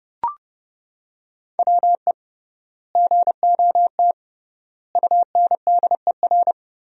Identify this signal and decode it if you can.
Morse code, heard as a keyed tone, about 32 wpm, 700 Hz. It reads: WE GOT UNDER
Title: 3x Word-Spacing - Sentences from Top 300 Words - 30wpm